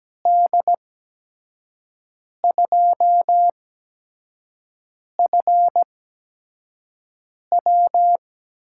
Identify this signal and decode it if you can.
Morse code, heard as a keyed tone, about 17 wpm, spaced as for 5 wpm, 700 Hz. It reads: D2FW
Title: USA Call Signs - 6 Characters - 8x Character Spacing - 17wpm